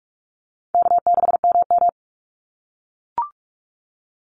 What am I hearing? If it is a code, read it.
Morse code, heard as a keyed tone, about 44 wpm, 700 Hz. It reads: K6MM
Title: Easy Call Signs - 45wpm